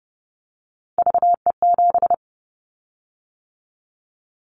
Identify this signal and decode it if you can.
Morse code, heard as a keyed tone, about 30 wpm, 700 Hz. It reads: VE7
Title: Callsign Prefixes for Top 50 DXCC - 30wpm